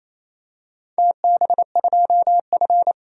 Morse code, heard as a keyed tone, about 29 wpm, 690 Hz. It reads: TB2F
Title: DXCC-100 Call Signs - 4 Characters - 28wpm